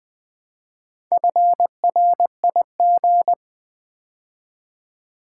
Morse code, heard as a keyed tone, about 20 wpm, 700 Hz. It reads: FRIG
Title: Build A Word - 2 Letters At A Time - 8-Letter Words - 20wpm